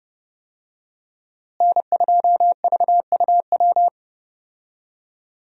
Morse code, heard as a keyed tone, about 30 wpm, 700 Hz. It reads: N2VUW
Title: USA Call Signs - 5 Characters - Repeated 3x - 30wpm